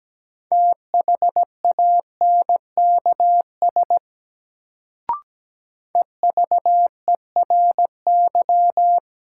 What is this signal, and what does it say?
Morse code, heard as a keyed tone, about 17 wpm, 700 Hz. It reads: THANKS EVERY
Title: Sets of 5 Words - New Word Drill - 17wpm